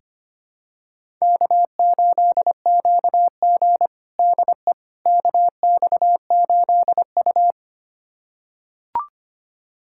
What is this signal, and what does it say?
Morse code, heard as a keyed tone, about 25 wpm, 700 Hz. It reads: K8QG DE KX8U
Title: USA Callsign DE Callsign - 4 Characters - 25wpm